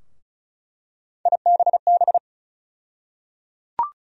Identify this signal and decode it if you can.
Morse code, heard as a keyed tone, about 34 wpm, 700 Hz. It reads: IBB
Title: USA Call Sign Postfixes - 3 Letters 35wpm